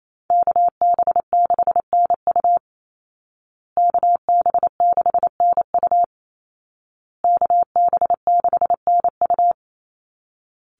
Morse code, heard as a keyed tone, about 28 wpm, 700 Hz. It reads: KB6NU KB6NU KB6NU
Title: Call Signs Repeated 2x - 28wpm